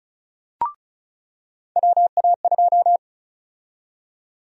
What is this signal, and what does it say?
Morse code, heard as a keyed tone, about 36 wpm, 700 Hz. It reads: WA2
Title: USA Call Sign Prefixes - 35wpm